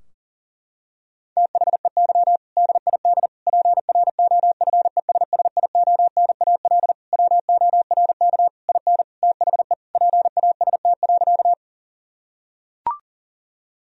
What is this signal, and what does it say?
Morse code, heard as a keyed tone, about 40 wpm, 700 Hz. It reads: THEY DID PROFESSIONAL WORK IN THE PAST.
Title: Sentences from Top 1000 Words - Review Part 4 - 40wpm